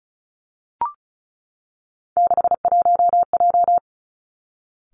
Morse code, heard as a keyed tone, about 36 wpm, 700 Hz. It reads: 61J
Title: Three Challenging Letter-Number Combinations 35wpm